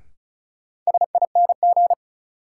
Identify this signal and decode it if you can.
Morse code, heard as a keyed tone, about 36 wpm, 700 Hz. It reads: SING
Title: Sentences from Top 400 Words - New Word Drill - 35wpm